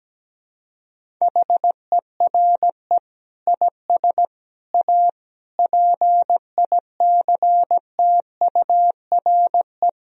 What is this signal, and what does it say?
Morse code, heard as a keyed tone, about 17 wpm, 700 Hz. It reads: HERE IS A PICTURE
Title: Sentences from Top 1000 Words - Review Part 1 - 17wpm